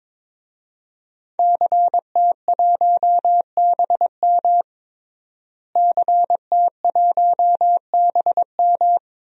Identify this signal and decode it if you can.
Morse code, heard as a keyed tone, about 22 wpm, 700 Hz. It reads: CT1BM CT1BM
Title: DXCC-100 Call Signs - 5 Characters - Repeated 2x - 22wpm